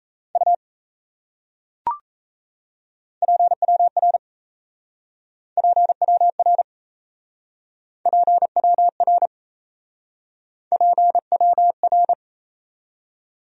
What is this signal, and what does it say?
Morse code, heard as a keyed tone, about 36 wpm, 700 Hz. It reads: U PWR PWR PWR PWR